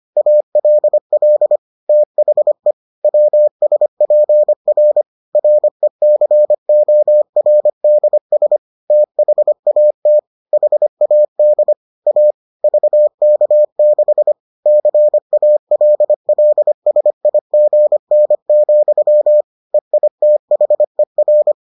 Morse code, heard as a keyed tone, about 25 wpm, 600 Hz. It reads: ALL THE WSPR RECORDS THAT HAD A VK6 CALLSIGN, EITHER